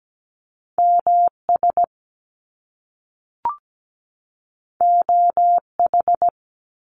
Morse code, heard as a keyed tone, about 17 wpm, 700 Hz. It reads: MS OH
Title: US States 17wpm